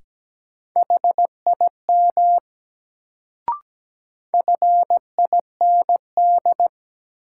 Morse code, heard as a keyed tone, about 17 wpm, 700 Hz. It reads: HIM FIND
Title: Sets of 2 Words - New Word Drill - 17wpm